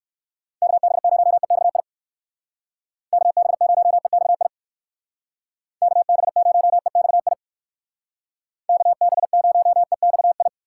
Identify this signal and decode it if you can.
Morse code, heard as a keyed tone, about 59 wpm, 700 Hz. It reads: KB0EXI KB0EXI KB0EXI KB0EXI